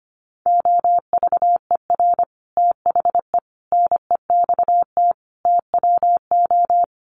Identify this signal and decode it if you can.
Morse code, heard as a keyed tone, about 25 wpm, 700 Hz. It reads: OVER THE NEXT TWO
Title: Sets of 5 Words Encore2 25wpm